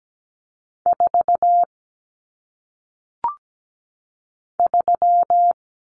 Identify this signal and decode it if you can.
Morse code, heard as a keyed tone, about 17 wpm, 700 Hz. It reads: 4 3
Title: Numbers - 1-Digit 17wpm